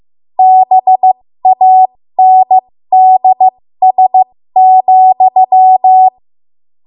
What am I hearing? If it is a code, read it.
Morse code, heard as a keyed tone, about 15 wpm, spaced as for 13 wpm, 750 Hz. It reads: BANDS,
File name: Morse / 13WPM